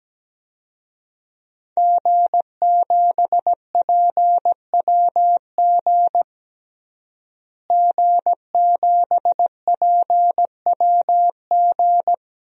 Morse code, heard as a keyed tone, about 17 wpm, 700 Hz. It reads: G7PWG G7PWG
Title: DXCC-100 Call Signs - 5 Characters - Repeated 3x - 17wpm